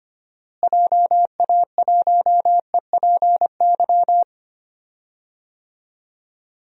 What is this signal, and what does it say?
Morse code, heard as a keyed tone, about 25 wpm, 700 Hz. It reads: JA1EPY
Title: DXCC-100 Call Signs - 6 Characters - Repeated 2x - 25wpm